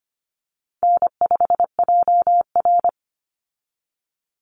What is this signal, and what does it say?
Morse code, heard as a keyed tone, about 25 wpm, 700 Hz. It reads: N5JR
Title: Call Signs Repeated 2x - 25wpm